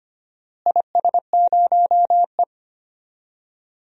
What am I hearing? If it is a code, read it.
Morse code, heard as a keyed tone, about 25 wpm, 700 Hz. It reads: IS0E
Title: DXCC-100 Call Signs - 4 Characters - Repeated 2x - 25wpm